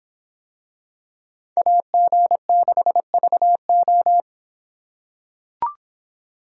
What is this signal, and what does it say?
Morse code, heard as a keyed tone, about 25 wpm, 700 Hz. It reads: AG6VO